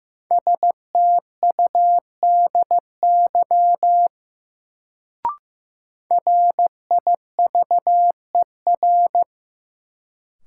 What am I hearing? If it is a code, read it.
Morse code, heard as a keyed tone, about 16 wpm, 700 Hz. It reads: STUDY RIVER
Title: Sentences from Top 300 Words - New Word Drill - 15wpm